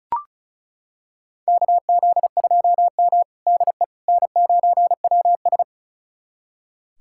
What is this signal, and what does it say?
Morse code, heard as a keyed tone, about 36 wpm, 700 Hz. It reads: KZ2M DE N9WS
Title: USA Callsign DE Callsign - 4 Characters - 35wpm